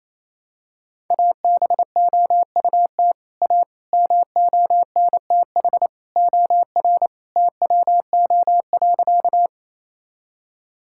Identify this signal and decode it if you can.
Morse code, heard as a keyed tone, about 27 wpm, 700 Hz. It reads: ABOUT A MONTH OR TWO.
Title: Sentences from Top 1000 Words - Review Part 5 - 28wpm